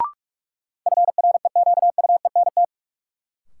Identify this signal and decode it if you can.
Morse code, heard as a keyed tone, about 44 wpm, 700 Hz. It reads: FREQUENT
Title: Sentences from Top 1400 Words - New Word Drill - 45wpm